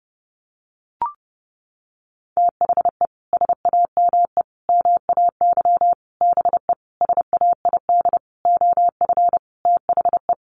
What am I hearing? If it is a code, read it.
Morse code, heard as a keyed tone, about 30 wpm, 700 Hz. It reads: THE SAME MAY BE SAID OF THE
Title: Sentences from Top 1100 Words 30wpm